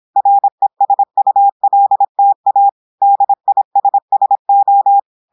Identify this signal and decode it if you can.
Morse code, heard as a keyed tone, about 26 wpm, 800 Hz. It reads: RESULTA DISSO
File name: audios / portugais / portugais texte en clair 1 vitesse 26 mots minutes